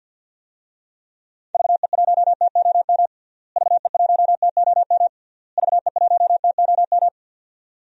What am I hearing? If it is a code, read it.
Morse code, heard as a keyed tone, about 46 wpm, 700 Hz. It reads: VE1TOM VE1TOM VE1TOM
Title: DXCC-100 Call Signs - 6 Characters - Repeated 3x - 50wpm